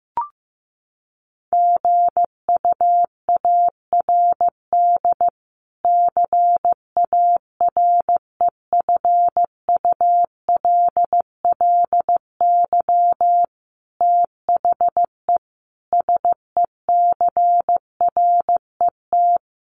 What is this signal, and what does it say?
Morse code, heard as a keyed tone, about 14 wpm, 700 Hz. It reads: GUARD CAREFULLY THE SECRET
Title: Sentences from Top 1300 Words 15wpm